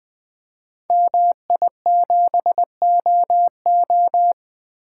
Morse code, heard as a keyed tone, about 20 wpm, 700 Hz. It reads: MI7OO
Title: Code Groups - 5 Letter:Numbers - 20wpm